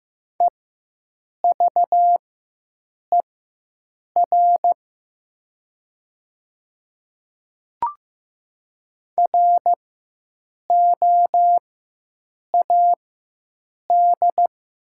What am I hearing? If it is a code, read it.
Morse code, heard as a keyed tone, about 15 wpm, spaced as for 7 wpm, 700 Hz. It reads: EVER ROAD
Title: 4x Character Spacing - Top 500 Words - 15wpm